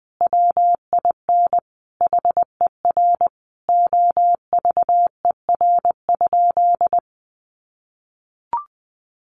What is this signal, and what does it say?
Morse code, heard as a keyed tone, about 19 wpm, 700 Hz. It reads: WIN HER OVER?
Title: Sentences from Top 1500 Words 20wpm